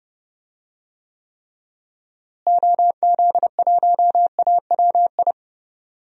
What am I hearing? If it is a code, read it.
Morse code, heard as a keyed tone, about 31 wpm, 700 Hz. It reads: OZ1AWI
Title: DXCC-100 Call Signs - 6 Characters - Repeated 3x - 30wpm